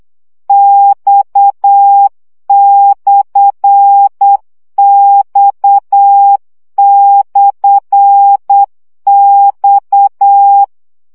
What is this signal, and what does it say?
Morse code, heard as a keyed tone, about 8 wpm, 800 Hz. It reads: X/X/X
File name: CW gyakorlas x per 7 baud